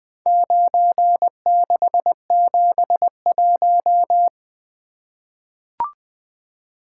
Morse code, heard as a keyed tone, about 20 wpm, 700 Hz. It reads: 9671
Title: Four Letter-Number Combinations - 20wpm